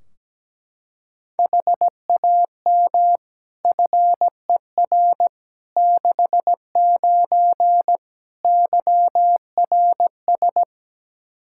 Morse code, heard as a keyed tone, about 17 wpm, 700 Hz. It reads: HAM FER 69 YRS
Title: K1USN SST QSO Practice 17wpm